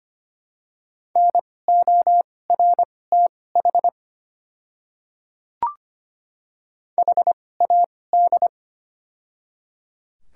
Morse code, heard as a keyed tone, about 25 wpm, spaced as for 18 wpm, 700 Hz. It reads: NORTH HAD
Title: 2x Character Spacing - Top 500 Words - 25wpm